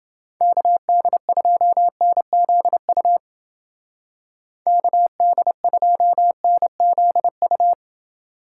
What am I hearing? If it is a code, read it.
Morse code, heard as a keyed tone, about 29 wpm, 700 Hz. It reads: KD2NZU KD2NZU